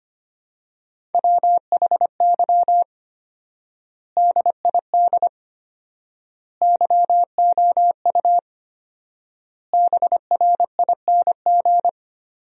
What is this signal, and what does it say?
Morse code, heard as a keyed tone, about 25 wpm, 700 Hz. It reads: WHY DID YOU BRING
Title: 4x Word-Spacing - Sentences from Top 500 Words - 25wpm